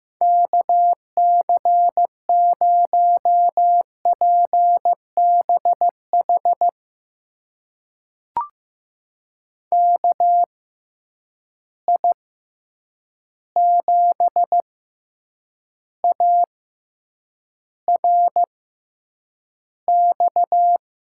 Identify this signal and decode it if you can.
Morse code, heard as a keyed tone, about 15 wpm, spaced as for 8 wpm, 700 Hz. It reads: KC0PBH KI7ARX
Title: USA Call Signs - 6 Characters - 6x Character Spacing - 15wpm